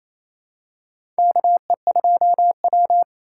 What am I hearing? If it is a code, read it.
Morse code, heard as a keyed tone, about 28 wpm, 700 Hz. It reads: KE2W
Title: USA Call Signs - 4 Characters - 4x Character Spacing - 28wpm